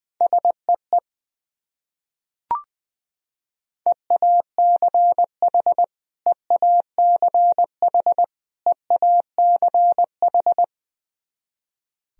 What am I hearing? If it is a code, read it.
Morse code, heard as a keyed tone, about 21 wpm, 700 Hz. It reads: SEE EACH EACH EACH
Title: Top 100 Words Repeated 20wpm